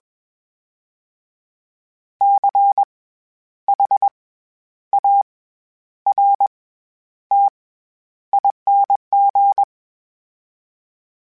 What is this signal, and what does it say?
Morse code, heard as a keyed tone, about 21 wpm, 800 Hz. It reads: C H A R T ING
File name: Special INGs-mp3s / Special INGs-mp3s / SlowToFaster ing @20